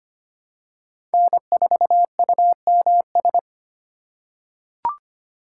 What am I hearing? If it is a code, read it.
Morse code, heard as a keyed tone, about 25 wpm, 700 Hz. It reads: N4UMS
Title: USA Call Signs - 5 Characters - 2x Character Spacing - 25wpm